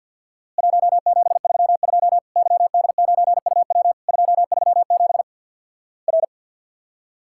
Morse code, heard as a keyed tone, about 50 wpm, 700 Hz. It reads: 1732 YD9UW 237 R
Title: Worked All Europe DX Contest - QTC Exchange with Repeats - 50wpm